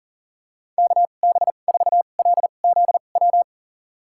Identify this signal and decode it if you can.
Morse code, heard as a keyed tone, about 40 wpm, spaced as for 29 wpm, 700 Hz. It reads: KB4LZW
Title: USA Call Signs - 6 Characters - 2x Character Spacing - 40wpm